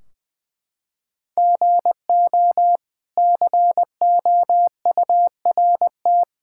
Morse code, heard as a keyed tone, about 20 wpm, 700 Hz. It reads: GO COURT
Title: Sentences from Top 1600 Words 20wpm